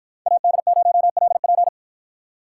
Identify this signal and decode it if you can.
Morse code, heard as a keyed tone, about 53 wpm, 700 Hz. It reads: AD0LP